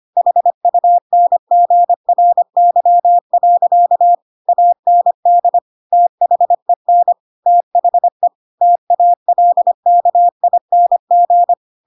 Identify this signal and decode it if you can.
Morse code, heard as a keyed tone, about 25 wpm, 700 Hz. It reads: HUNGRY. AND THEN THE TALKING
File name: Pinocchio / Pinocchio ch7 25 25 ik1hge